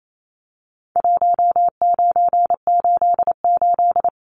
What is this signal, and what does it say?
Morse code, heard as a keyed tone, about 28 wpm, 700 Hz. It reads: 1988
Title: Numbers - Contest Style - 4-Digit 28wpm